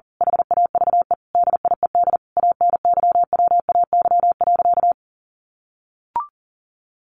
Morse code, heard as a keyed tone, about 40 wpm, 700 Hz. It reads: HAVE DIED ANYWAY.
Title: A Sound Of Thunder 40wpm